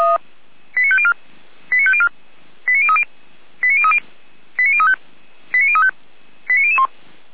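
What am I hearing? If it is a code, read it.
Five-tone selective call (ZVEI), heard as quick runs of five tones, tones of about 70 ms. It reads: 89504 89504 89030 89030 89035 89035 89001
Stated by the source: Five-tone ZVEI selective call received by SDR, from audio.f85360.feuerwehr-orlamuende.org (Feuerwehr Orlamünde).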